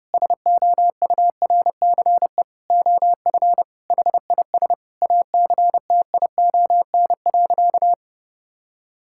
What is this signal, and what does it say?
Morse code, heard as a keyed tone, about 30 wpm, 700 Hz. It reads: SOURCE OF HIS ACTION.
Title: Sentences from Top 1000 Words - Review Part 5 - 30wpm